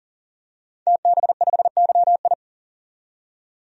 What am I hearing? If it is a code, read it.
Morse code, heard as a keyed tone, about 40 wpm, 700 Hz. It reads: TB5YI
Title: DXCC-100 Call Signs - 5 Characters - 40wpm